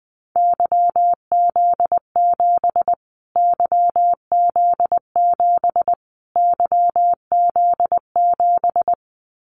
Morse code, heard as a keyed tone, about 20 wpm, 700 Hz. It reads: YZ7 YZ7 YZ7
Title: Callsign Prefixes for Top 25 DXCC - Repeated 3x - 20wpm